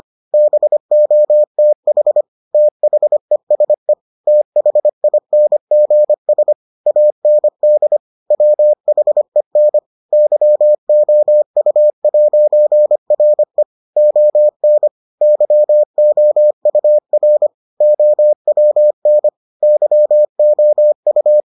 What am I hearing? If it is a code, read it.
Morse code, heard as a keyed tone, about 25 wpm, 600 Hz. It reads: BOTH THESE THINGS AND WHEN YOU'RE ON YOUR OWN YOU